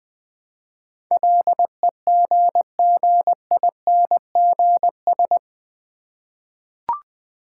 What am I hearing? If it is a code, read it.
Morse code, heard as a keyed tone, about 20 wpm, 700 Hz. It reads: LEGGINGS
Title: Difficult Words with Hints - 20wpm